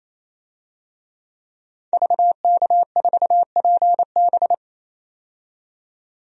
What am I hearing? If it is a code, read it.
Morse code, heard as a keyed tone, about 28 wpm, 700 Hz. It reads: VK4PB
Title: DXCC-100 Call Signs - 5 Characters - 28wpm